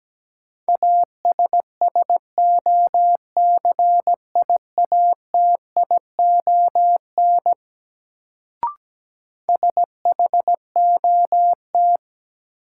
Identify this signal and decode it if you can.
Morse code, heard as a keyed tone, about 17 wpm, 700 Hz. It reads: ASSOCIATION SHOT
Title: Sentences from Top 1100 Words - New Word Drill Alt - 17wpm